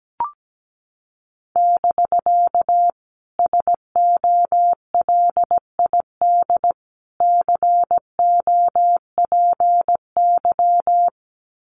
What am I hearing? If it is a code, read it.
Morse code, heard as a keyed tone, about 17 wpm, 700 Hz. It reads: <BK> SOLID COPY
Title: Common QSO Phrases Encore 17wpm